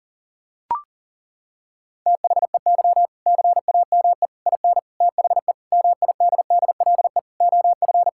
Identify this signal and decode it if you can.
Morse code, heard as a keyed tone, about 40 wpm, 700 Hz. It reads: THEY CAME IN THE MIDDLE OF
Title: Sentences from Top 1000 Words - Review Part 6 - 40wpm